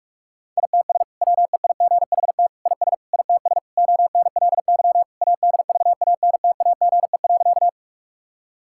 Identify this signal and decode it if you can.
Morse code, heard as a keyed tone, about 42 wpm, 700 Hz. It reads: ITS WEIGHT IS ITS ONLY ADVANTAGE.
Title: Sentences from Top 1000 Words - Review Part 6 - 45wpm